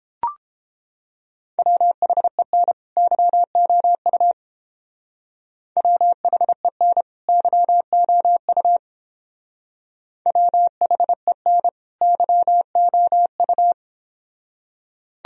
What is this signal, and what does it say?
Morse code, heard as a keyed tone, about 29 wpm, 700 Hz. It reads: WHEN YOU WHEN YOU WHEN YOU